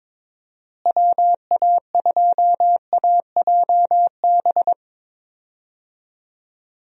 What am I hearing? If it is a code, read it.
Morse code, heard as a keyed tone, about 22 wpm, 700 Hz. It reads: WA2AJB
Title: USA Call Signs - 6 Characters - Repeated 3x - 22wpm